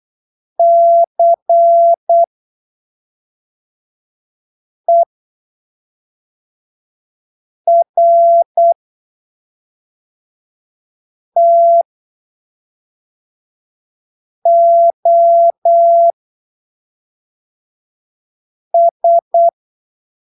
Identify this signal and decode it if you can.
Morse code, heard as a keyed tone, about 8 wpm, spaced as for 3 wpm, 680 Hz. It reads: CERTOS
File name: arquivos / texto / texto 15 08ppm